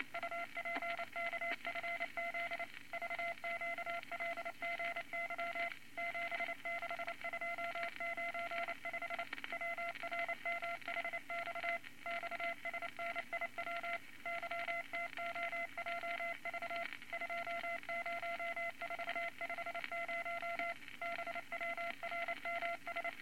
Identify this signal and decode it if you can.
Morse code, heard as a keyed tone, about 29 wpm, 680 Hz, with noise. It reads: UPKFZ VQLGY 76295 WRMHX XSNIW YTOJV 20450 BWRMS
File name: Morse Code Records / 22 A LCM 172 Twenty Words Per Minute